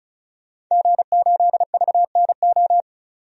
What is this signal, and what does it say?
Morse code, heard as a keyed tone, about 36 wpm, 700 Hz. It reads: G8VNO
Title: DXCC-100 Call Signs - 5 Characters - 35wpm